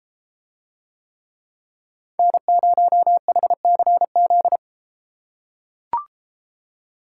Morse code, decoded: N0HCZ

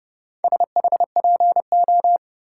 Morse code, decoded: SHPO

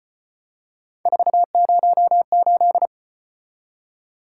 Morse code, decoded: 408

